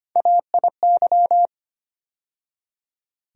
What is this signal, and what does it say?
Morse code, heard as a keyed tone, about 25 wpm, 700 Hz. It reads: AIY
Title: USA Call Sign Postfixes - 3 Letters 25wpm